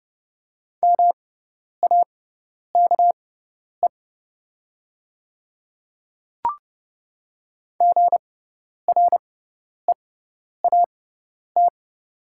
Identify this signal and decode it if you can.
Morse code, heard as a keyed tone, about 30 wpm, spaced as for 10 wpm, 700 Hz. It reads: MAKE GREAT
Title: 6x Character Spacing - Top 300 Words - 30wpm